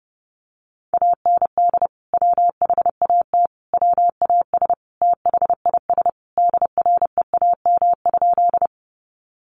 Morse code, heard as a keyed tone, about 30 wpm, 700 Hz. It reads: AND WHAT WAS THIS DREAM?
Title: Sentences from Top 1500 Words 30wpm